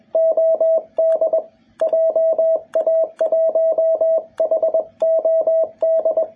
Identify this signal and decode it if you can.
Morse code, heard as a keyed tone, about 21 wpm, 650 Hz, with radio noise. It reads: OD JA1HOD
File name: jh8cbh / 2021cw